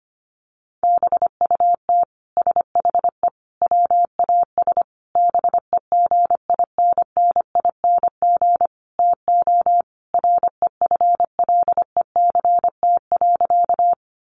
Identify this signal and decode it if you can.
Morse code, heard as a keyed tone, about 25 wpm, 700 Hz. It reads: BUT SHE WAS BEGINNING TO REFLECT.
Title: Sentences from Top 1500 Words 25wpm